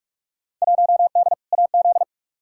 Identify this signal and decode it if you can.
Morse code, heard as a keyed tone, about 44 wpm, 700 Hz. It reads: 1D AZ
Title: Field Day Exchanges - 45wpm